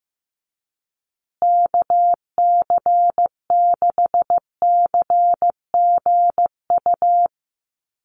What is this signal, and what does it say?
Morse code, heard as a keyed tone, about 15 wpm, 700 Hz. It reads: KC6CGU
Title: USA Call Signs Repeated 2x 15wpm